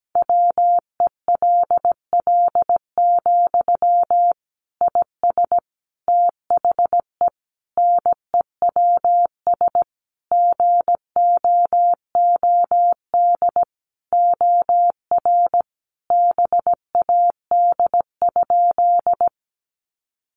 Morse code, heard as a keyed tone, about 17 wpm, 700 Hz. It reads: WELL, IS THE NEWS GOOD OR BAD?
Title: Sentences from Top 800 Words 17wpm